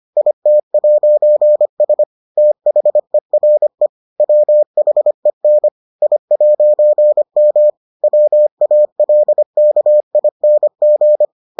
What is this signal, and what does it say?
Morse code, heard as a keyed tone, about 25 wpm, 600 Hz. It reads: IT'S THERE WHEN I'M WALKING